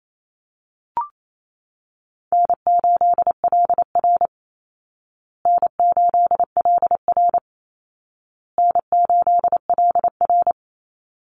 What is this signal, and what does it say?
Morse code, heard as a keyed tone, about 28 wpm, 700 Hz. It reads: N8LR N8LR N8LR
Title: Easy Call Signs Repeated 3x - 28wpm